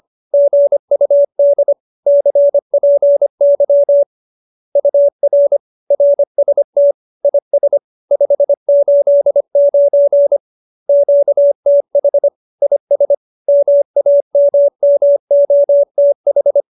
Morse code, heard as a keyed tone, about 25 wpm, 570 Hz. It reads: GUD CPY UR RST IS 589 QTH IS MAMMOTH